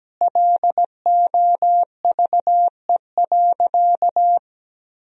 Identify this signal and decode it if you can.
Morse code, heard as a keyed tone, about 17 wpm, 700 Hz. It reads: LOVE.
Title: Sentences from Top 400 Words and Abbreviations 17wpm